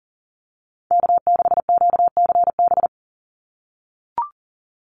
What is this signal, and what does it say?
Morse code, heard as a keyed tone, about 40 wpm, 700 Hz. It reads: K6QCB
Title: NA SOTA Chaser Callsigns 40wpm